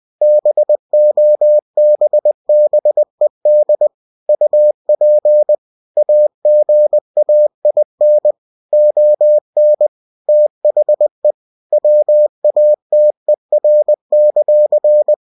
Morse code, heard as a keyed tone, about 20 wpm, 600 Hz. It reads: BOBBED UP AGAIN ON THE WATER;
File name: cw-books / winnie-the-pooh / winnie-20wpm-page-0092